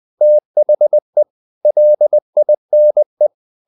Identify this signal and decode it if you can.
Morse code, heard as a keyed tone, about 20 wpm, 600 Hz. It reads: THE LINE
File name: the old man and the sea / the old man and the sea part 1 20wpm